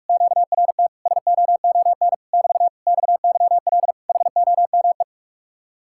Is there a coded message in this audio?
Yes: Morse code from 0.1 to 5.0 s, heard as a keyed tone, about 50 wpm, 700 Hz. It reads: QRT SOON <BT> XYL HOME